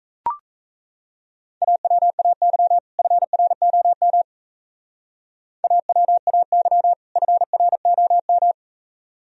Morse code, heard as a keyed tone, about 34 wpm, 700 Hz. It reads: AWAY FROM AWAY FROM
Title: Speed-Racing - Sets of 2 Words 28wpm